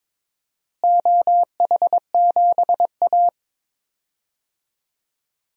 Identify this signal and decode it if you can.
Morse code, heard as a keyed tone, about 22 wpm, 700 Hz. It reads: OH7A
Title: DXCC-100 Call Signs - 4 Characters - Repeated 3x - 22wpm